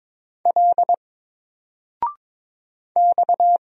Morse code, heard as a keyed tone, about 22 wpm, 700 Hz. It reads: L X
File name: Single-Letter-Number-22wpm